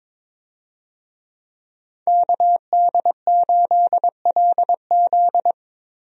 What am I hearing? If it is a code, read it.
Morse code, heard as a keyed tone, about 22 wpm, 700 Hz. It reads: KD8LZ
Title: USA Call Signs - 5 Characters - Repeated 3x - 22wpm